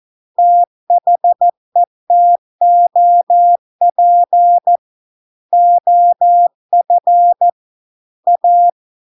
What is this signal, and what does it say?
Morse code, heard as a keyed tone, about 14 wpm, 700 Hz. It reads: THE TOP OF A